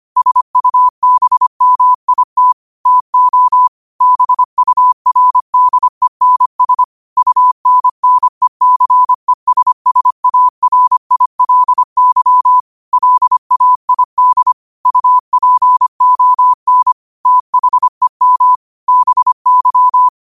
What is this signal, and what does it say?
Morse code, heard as a keyed tone, about 25 wpm, 1000 Hz. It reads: SUBMIT TO BURDENS UNNECESSARILY LAID UPON THEM BY
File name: qotdfiles / 25wpm